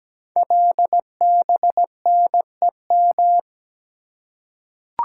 Morse code, heard as a keyed tone, about 17 wpm, 700 Hz. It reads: LBNEM
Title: Code Groups - 5 Letters - 17wpm